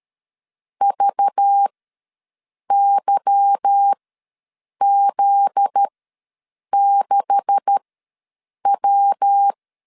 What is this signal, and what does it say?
Morse code, heard as a keyed tone, about 13 wpm, spaced as for 7 wpm, 800 Hz. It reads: VYZ6W
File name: LEZIONI / 9 286 ws 12wpm 3351